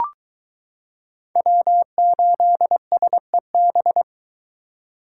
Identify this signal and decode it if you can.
Morse code, heard as a keyed tone, about 22 wpm, 700 Hz. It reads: W8SEB